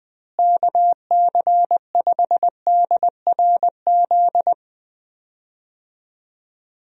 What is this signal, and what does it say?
Morse code, heard as a keyed tone, about 20 wpm, 700 Hz. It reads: KC5DRZ
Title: USA Call Signs - 6 Characters - Repeated 2x - 20wpm